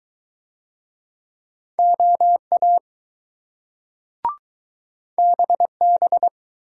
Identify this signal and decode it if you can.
Morse code, heard as a keyed tone, about 20 wpm, 700 Hz. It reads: OA BB